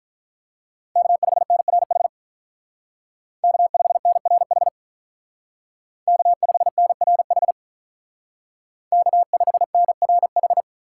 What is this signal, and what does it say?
Morse code, heard as a keyed tone, about 46 wpm, 700 Hz. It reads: K5NRH K5NRH K5NRH K5NRH